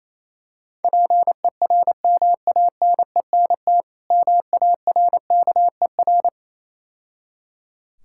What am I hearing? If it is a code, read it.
Morse code, heard as a keyed tone, about 27 wpm, 690 Hz. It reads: PERMANENT MARKER
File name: Easy Words - Speed-Racing with Hints - 22wpm